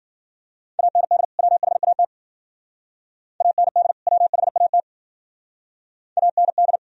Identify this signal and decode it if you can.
Morse code, heard as a keyed tone, about 53 wpm, 700 Hz. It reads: AND WHAT AND WHAT AND